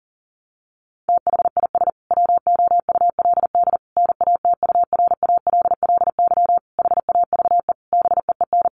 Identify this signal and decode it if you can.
Morse code, heard as a keyed tone, about 40 wpm, 700 Hz. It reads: THIS WOULD NATURALLY HAVE BEEN
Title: Sentences from Top 1300 Words 40wpm